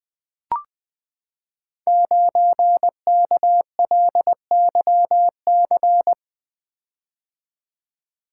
Morse code, heard as a keyed tone, about 20 wpm, 700 Hz. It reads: 9KLYC
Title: Code Groups - 5 Letter:Numbers - 20wpm